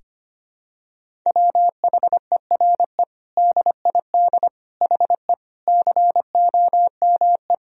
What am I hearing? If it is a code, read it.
Morse code, heard as a keyed tone, about 25 wpm, 700 Hz. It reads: WHERE DID HE COME
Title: Sentences from Top 200 Words and Abbreviations 25wpm